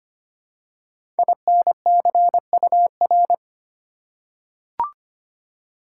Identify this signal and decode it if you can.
Morse code, heard as a keyed tone, about 25 wpm, 700 Hz. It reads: INCUR
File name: 5 Letter Words - First 3 Letters Given - 25wpm